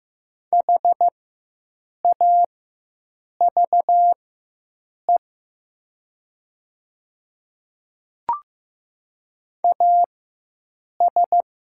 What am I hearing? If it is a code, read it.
Morse code, heard as a keyed tone, about 15 wpm, spaced as for 7 wpm, 700 Hz. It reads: HAVE AS
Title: 4x Character Spacing - Top 100 Words - 15wpm